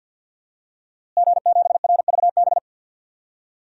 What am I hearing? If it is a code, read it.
Morse code, heard as a keyed tone, about 51 wpm, 700 Hz. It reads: G7RVB
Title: DXCC-100 Call Signs - 5 Characters - Repeated 2x - 50wpm